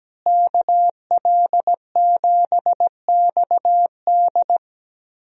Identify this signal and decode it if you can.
Morse code, heard as a keyed tone, about 17 wpm, 700 Hz. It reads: KL7XD
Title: DXCC-100 Call Signs - 5 Characters - Repeated 3x - 17wpm